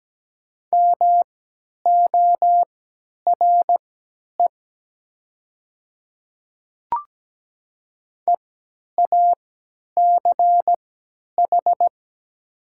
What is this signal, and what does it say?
Morse code, heard as a keyed tone, about 17 wpm, spaced as for 10 wpm, 700 Hz. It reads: MORE EACH
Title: 3x Character Spacing - Top 100 Words - 17wpm